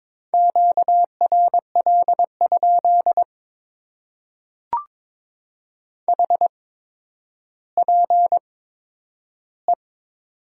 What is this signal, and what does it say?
Morse code, heard as a keyed tone, about 22 wpm, spaced as for 11 wpm, 700 Hz. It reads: QRL? HPE
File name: Top 90 QSO Elements - 8x Character Spacing - 22wpm